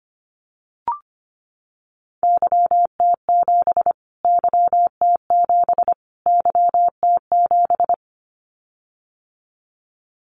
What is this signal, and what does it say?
Morse code, heard as a keyed tone, about 25 wpm, 700 Hz. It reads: YT7 YT7 YT7
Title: Callsign Prefixes for Top 25 DXCC - Repeated 3x - 25wpm